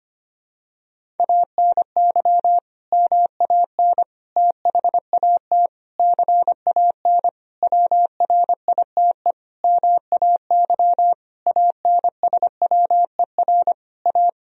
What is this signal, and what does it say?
Morse code, heard as a keyed tone, about 25 wpm, 700 Hz. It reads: ANY MAN THAT CAN WRITE MAY ANSWER A
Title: Sentences from Top 1000 Words - Review Part 1 - 25wpm